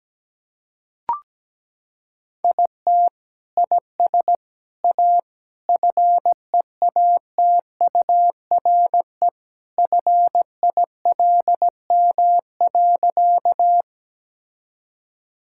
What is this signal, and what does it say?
Morse code, heard as a keyed tone, about 17 wpm, 700 Hz. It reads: IT IS A FEATURE FILM.
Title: Sentences from Top 1000 Words - Review Part 5 - 17wpm